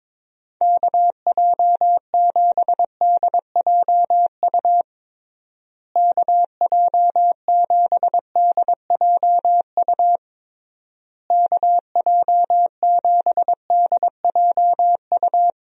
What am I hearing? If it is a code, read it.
Morse code, heard as a keyed tone, about 22 wpm, 700 Hz. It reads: KJ7DJU KJ7DJU KJ7DJU
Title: USA Call Signs - 6 Characters - Repeated 3x - 22wpm